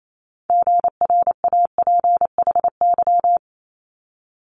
Morse code, heard as a keyed tone, about 28 wpm, 700 Hz. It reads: GRAPHY
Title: Multi-letter Phonemes Quick Fire 28wpm